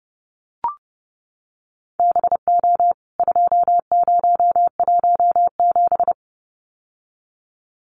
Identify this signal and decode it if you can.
Morse code, heard as a keyed tone, about 29 wpm, 700 Hz. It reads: BO 2017
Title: CWT Exchange 30wpm